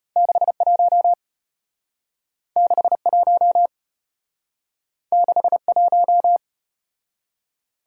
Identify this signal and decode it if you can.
Morse code, heard as a keyed tone, about 32 wpm, 700 Hz. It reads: 61 61 61